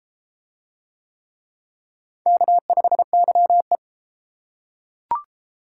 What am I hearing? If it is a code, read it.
Morse code, heard as a keyed tone, about 32 wpm, 700 Hz. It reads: K5YE